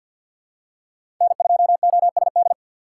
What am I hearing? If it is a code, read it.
Morse code, heard as a keyed tone, about 46 wpm, 700 Hz. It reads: N2OSD
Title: USA Call Signs - 5 Characters - 3x Character Spacing - 50wpm